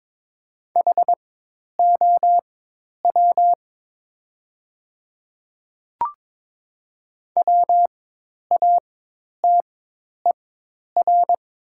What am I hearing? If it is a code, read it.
Morse code, heard as a keyed tone, about 22 wpm, spaced as for 10 wpm, 700 Hz. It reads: HOW WATER
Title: 4x Character Spacing - Top 100 Words - 22wpm